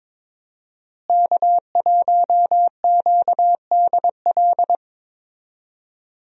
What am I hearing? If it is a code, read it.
Morse code, heard as a keyed tone, about 22 wpm, 700 Hz. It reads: K1QDL
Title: USA Call Signs - 5 Characters - Repeated 3x - 22wpm